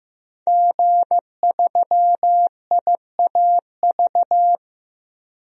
Morse code, heard as a keyed tone, about 15 wpm, 700 Hz. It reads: G3IAV
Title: DXCC-100 Call Signs - 5 Characters - 15wpm